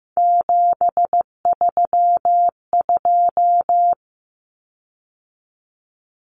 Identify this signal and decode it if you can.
Morse code, heard as a keyed tone, about 15 wpm, 700 Hz. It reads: 732
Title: Common Cut Numbers - 3-Digit 15wpm